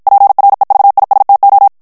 Morse code, heard as a keyed tone, about 52 wpm, 800 Hz. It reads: PREVISTO
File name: morse mp3 / ESPAGNOL TEXTE EN CLAIR 2 VITESSE 50 MOTS MINUTES